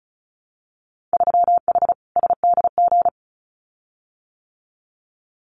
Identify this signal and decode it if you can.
Morse code, heard as a keyed tone, about 36 wpm, 700 Hz. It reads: 3H SDG
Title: Winter Field Day Exchanges Encore 35wpm